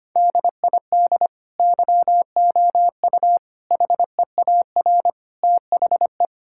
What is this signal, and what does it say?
Morse code, heard as a keyed tone, about 25 wpm, 700 Hz. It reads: DID YOU HEAR THE
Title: Sentences from Top 1000 Words - Review Part 5 - 25wpm